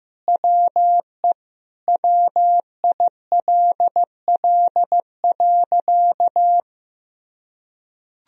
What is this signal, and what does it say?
Morse code, heard as a keyed tone, about 15 wpm, 700 Hz. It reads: WE WILL.
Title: Sentences from Top 1000 Words - Review Part 1 - 15wpm